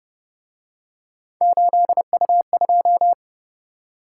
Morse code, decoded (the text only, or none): 8U2